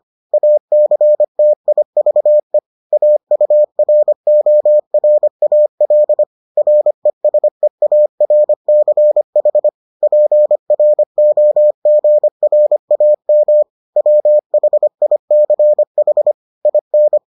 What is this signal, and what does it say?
Morse code, heard as a keyed tone, about 25 wpm, 600 Hz. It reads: ACTIVE AURORAL RESEARCH PROGRAM WHICH IN